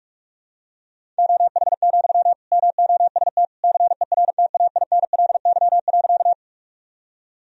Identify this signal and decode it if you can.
Morse code, heard as a keyed tone, about 47 wpm, 700 Hz. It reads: OH, MOST CERTAINLY.
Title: Sentences from Top 1000 Words - Review Part 1 - 45wpm